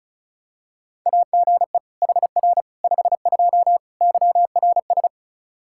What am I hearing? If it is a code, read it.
Morse code, heard as a keyed tone, about 36 wpm, 700 Hz. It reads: AGE HR 52 YRS
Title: Practice QSO - Exchange 3 - More Info - 35wpm